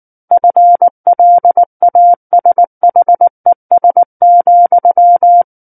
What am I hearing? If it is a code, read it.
Morse code, heard as a keyed tone, about 19 wpm, 700 Hz. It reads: FLASHES,
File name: Book One / 10-Chapter 10 - 19 WPM